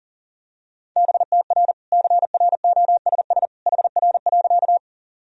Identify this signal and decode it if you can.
Morse code, heard as a keyed tone, about 40 wpm, 700 Hz. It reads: BTR CROSS HR.
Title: Sentences from Top 400 Words and Abbreviations 40wpm